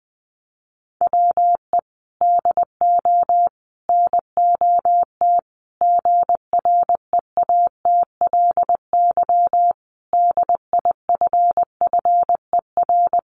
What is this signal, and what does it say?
Morse code, heard as a keyed tone, about 20 wpm, 700 Hz. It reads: WE DO NOT GREATLY DIFFER
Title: Sentences from Top 1400 Words 20wpm